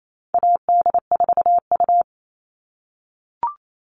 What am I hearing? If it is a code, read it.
Morse code, heard as a keyed tone, about 28 wpm, 700 Hz. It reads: AD4U
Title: All Cut Numbers - 4-Digit 28wpm